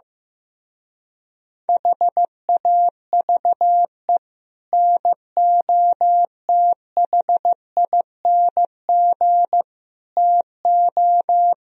Morse code, heard as a keyed tone, about 15 wpm, 700 Hz. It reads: HAVE NOTHING TO
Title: Speed-Racing - Sets of 5 Words 15wpm